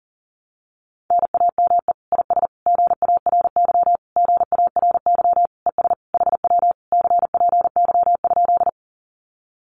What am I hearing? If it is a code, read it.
Morse code, heard as a keyed tone, about 40 wpm, 700 Hz. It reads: NAME IS GARY GARY ES HW CPY?
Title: Common QSO Phrases Encore 40wpm